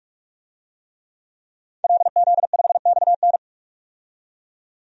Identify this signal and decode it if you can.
Morse code, heard as a keyed tone, about 44 wpm, 690 Hz. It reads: LZ5XN
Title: DXCC-100 Call Signs - 5 Characters - 45wpm